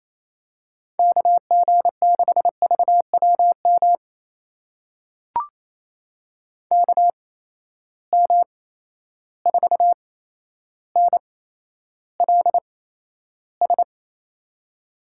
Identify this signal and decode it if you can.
Morse code, heard as a keyed tone, about 28 wpm, spaced as for 12 wpm, 700 Hz. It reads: KG6VWM KM4NLS
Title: USA Call Signs - 6 Characters - 8x Character Spacing - 28wpm